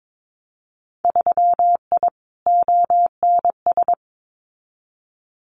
Morse code, heard as a keyed tone, about 22 wpm, 700 Hz. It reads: 3I ONS
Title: Winter Field Day Exchanges Encore 22wpm